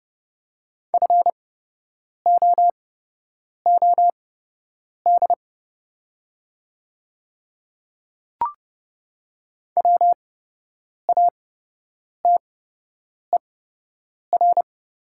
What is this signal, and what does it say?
Morse code, heard as a keyed tone, about 30 wpm, spaced as for 8 wpm, 700 Hz. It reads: FOOD WATER
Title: 8x Character Spacing - Top 500 Words - 30wpm